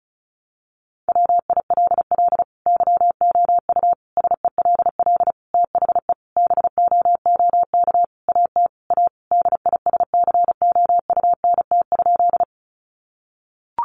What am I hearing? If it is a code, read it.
Morse code, heard as a keyed tone, about 34 wpm, 700 Hz. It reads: WILL YOU SELL THE BOOK AT A DISCOUNT?
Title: Sentences from Top 1200 Words 35wpm